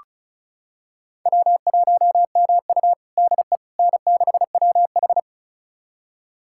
Morse code, heard as a keyed tone, about 36 wpm, 700 Hz. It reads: W1MU DE N6WH
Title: USA Callsign DE Callsign - 4 Characters - 35wpm